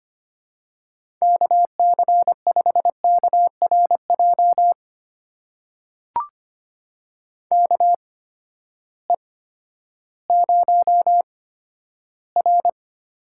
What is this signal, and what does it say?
Morse code, heard as a keyed tone, about 25 wpm, spaced as for 12 wpm, 700 Hz. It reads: KC5KRJ KE0R
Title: USA Call Signs - 6 Characters - 8x Character Spacing - 25wpm